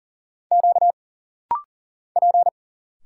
Morse code, heard as a keyed tone, about 42 wpm, 700 Hz. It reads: Q P
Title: Single Letters - Rapid-Fire 40wpm